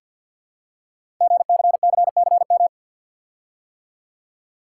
Morse code, heard as a keyed tone, about 51 wpm, 700 Hz. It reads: GCCCM
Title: Code Groups - 5 Letters - 50wpm